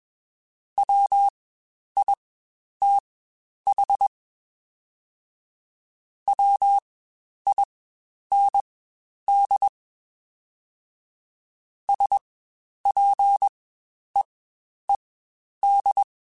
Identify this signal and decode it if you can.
Morse code, heard as a keyed tone, about 21 wpm, spaced as for 10 wpm, 800 Hz. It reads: WITH WIND SPEED
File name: Spaced Characters / Spaced Characters-mp3s / Weather5 22 sp @20